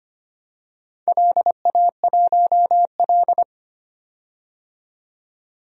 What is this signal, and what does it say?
Morse code, heard as a keyed tone, about 25 wpm, 700 Hz. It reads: LA1L
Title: DXCC-100 Call Signs - 4 Characters - 25wpm